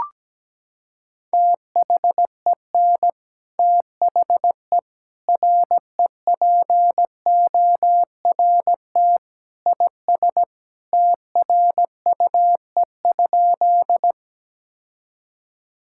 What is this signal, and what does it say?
Morse code, heard as a keyed tone, about 17 wpm, 700 Hz. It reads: THEN THE REPORT IS TRUE?
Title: Sentences from Top 1000 Words - Review Part 6 - 17wpm